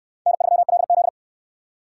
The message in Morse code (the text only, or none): N3LZ